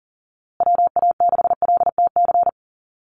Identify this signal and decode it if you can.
Morse code, heard as a keyed tone, about 40 wpm, 700 Hz. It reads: WA6LTC
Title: USA Call Signs Repeated 3x 40wpm